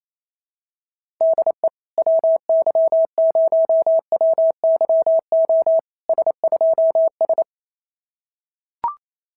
Morse code, heard as a keyed tone, about 29 wpm, 650 Hz. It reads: DE WY0WYO S2S
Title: SOTA QSO Practice 28wpm